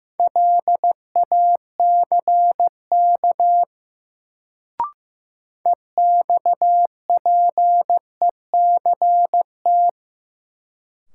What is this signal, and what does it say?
Morse code, heard as a keyed tone, about 15 wpm, 700 Hz. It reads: LACK EXPECT
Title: Sentences from Top 1100 Words - New Word Drill - 15wpm